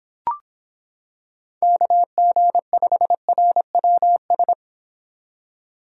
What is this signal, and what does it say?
Morse code, heard as a keyed tone, about 26 wpm, 700 Hz. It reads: KG5RWS